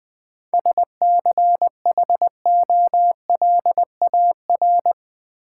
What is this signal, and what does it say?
Morse code, heard as a keyed tone, about 20 wpm, 700 Hz. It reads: SCHOLAR
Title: 7 Letter Words - First 3 Letters Given - 20wpm